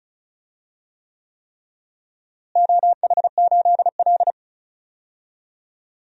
Morse code, heard as a keyed tone, about 34 wpm, 700 Hz. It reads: OH8L
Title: DXCC-100 Call Signs - 4 Characters - Repeated 3x - 35wpm